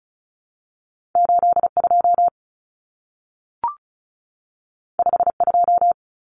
Morse code, heard as a keyed tone, about 36 wpm, 700 Hz. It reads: 82 52
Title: Common Cut Numbers - 2-Digit 35wpm